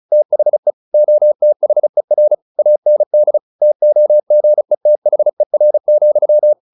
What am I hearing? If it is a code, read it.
Morse code, heard as a keyed tone, about 34 wpm, 600 Hz. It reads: THE OTHER AND TOGETHER,
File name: the old man and the sea 2 35wpm